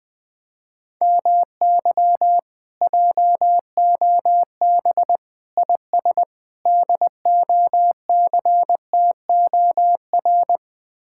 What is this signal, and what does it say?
Morse code, heard as a keyed tone, about 20 wpm, 700 Hz. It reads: MY JOB IS DOCTOR
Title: Speed-Racing Common QSO Phrases Encore - 17wpm